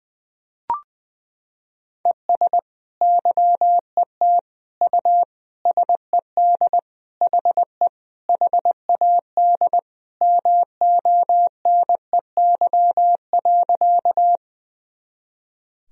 Morse code, heard as a keyed tone, about 20 wpm, 700 Hz. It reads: ES YET U SED HE HAD MONEY.
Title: Sentences from Top 500 Words and Abbreviations 20wpm